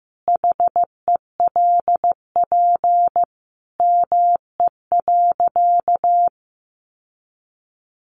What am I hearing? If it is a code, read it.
Morse code, heard as a keyed tone, about 15 wpm, 700 Hz. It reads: HELP ME.
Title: Sentences from Top 1300 Words 15wpm